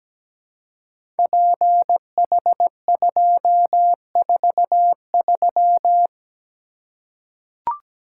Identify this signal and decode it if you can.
Morse code, heard as a keyed tone, about 17 wpm, 700 Hz. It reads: PH243
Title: Code Groups - 5 Letter:Numbers - 17wpm